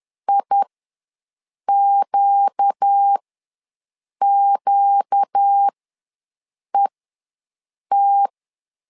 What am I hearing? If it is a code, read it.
Morse code, heard as a keyed tone, about 10 wpm, spaced as for 6 wpm, 800 Hz. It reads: IQQET
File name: LEZIONI / 4 Q ws 10wpm 2306